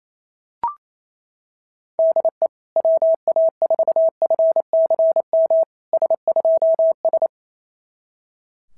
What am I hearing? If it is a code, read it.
Morse code, heard as a keyed tone, about 25 wpm, 650 Hz. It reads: DE WA4FCM S2S
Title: SOTA QSO Practice 28wpm